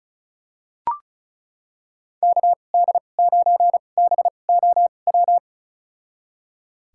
Morse code, heard as a keyed tone, about 34 wpm, spaced as for 25 wpm, 700 Hz. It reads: KD9BOW
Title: USA Call Signs - 2x Character Spacing - 35wpm